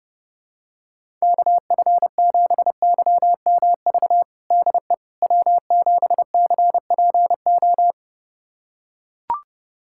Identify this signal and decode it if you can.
Morse code, heard as a keyed tone, about 30 wpm, 690 Hz. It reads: KF7YMV DE W7CPO
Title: USA Callsign DE Callsign 30wpm